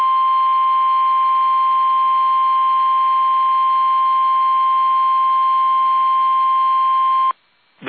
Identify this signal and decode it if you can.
Emergency Alert System (SAME) tones (the 1050 Hz attention tone). It is heard from 0.9 to 7.3 s.